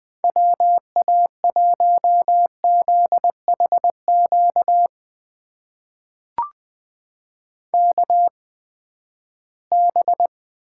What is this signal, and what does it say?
Morse code, heard as a keyed tone, about 21 wpm, spaced as for 14 wpm, 700 Hz. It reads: WA1ZHQ KB
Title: USA Call Signs - 6 Characters - 8x Character Spacing - 20wpm